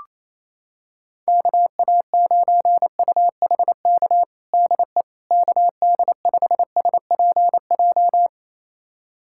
Morse code, heard as a keyed tone, about 28 wpm, 700 Hz. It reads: KA9UHK DE KD5SPJ
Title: USA Callsign DE Callsign - 6 Characters - 28wpm